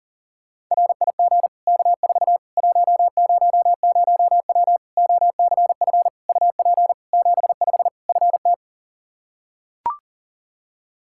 Morse code, heard as a keyed tone, about 40 wpm, 700 Hz. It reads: RIG K4 100W OCF UP 75 FT
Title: Practice QSO - Exchange 2 - Info - 40wpm